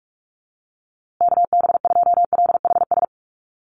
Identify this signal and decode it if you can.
Morse code, heard as a keyed tone, about 44 wpm, 700 Hz. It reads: KB2LHS
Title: USA Call Signs Repeated 3x 45wpm